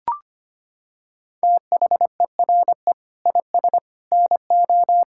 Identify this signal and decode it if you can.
Morse code, heard as a keyed tone, about 25 wpm, 700 Hz. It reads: THERE IS NO
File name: Sentences with 2-Word Phrases - 25wpm